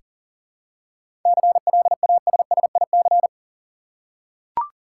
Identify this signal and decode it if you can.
Morse code, heard as a keyed tone, about 40 wpm, 700 Hz. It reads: CLASSIC
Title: Seven Letter Words - 40wpm